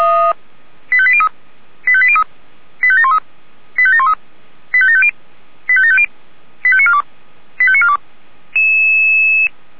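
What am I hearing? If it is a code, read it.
Five-tone selective call (ZVEI), heard as quick runs of five tones, tones of about 70 ms. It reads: 85803 85803 85713 85713 85750 85750 85942 85942
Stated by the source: ZVEI five-tone selective call received by SDR on channel 464.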